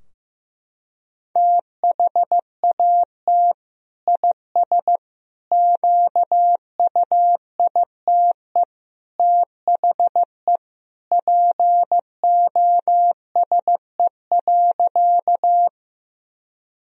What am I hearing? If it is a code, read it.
Morse code, heard as a keyed tone, about 15 wpm, 700 Hz. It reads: THAT IS QUITE THE POSE.
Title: Sentences from Top 1000 Words - Review Part 2 - 15wpm